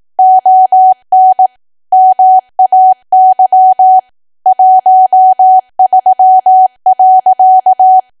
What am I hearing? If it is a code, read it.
Morse code, heard as a keyed tone, about 18 wpm, 750 Hz. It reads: ON MAY 13.
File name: Archive / 18 WPM / 18WPM